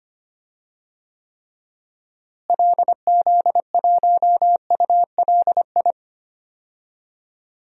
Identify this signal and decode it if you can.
Morse code, heard as a keyed tone, about 25 wpm, 700 Hz. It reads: LZ1ULI
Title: DXCC-100 Call Signs - 6 Characters - Repeated 2x - 25wpm